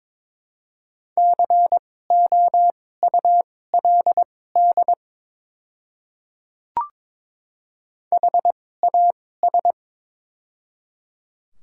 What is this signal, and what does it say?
Morse code, heard as a keyed tone, about 22 wpm, spaced as for 16 wpm, 700 Hz. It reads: COULD HAS
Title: 2x Character Spacing - Top 100 Words - 22wpm